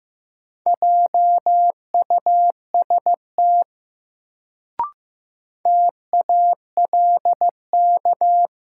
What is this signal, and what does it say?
Morse code, heard as a keyed tone, about 15 wpm, 700 Hz. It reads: JUST TALK
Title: Sets of 2 Words - New Word Drill - 15wpm